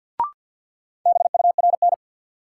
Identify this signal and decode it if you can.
Morse code, heard as a keyed tone, about 53 wpm, 700 Hz. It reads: BURN
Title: Sentences from Top 1300 Words - New Word Drill - 50wpm